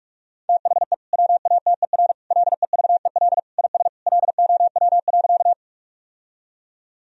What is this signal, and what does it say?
Morse code, heard as a keyed tone, about 46 wpm, 700 Hz. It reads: THE WATER LEVEL IS LOW.
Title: Sentences from Top 1000 Words - Review Part 4 - 45wpm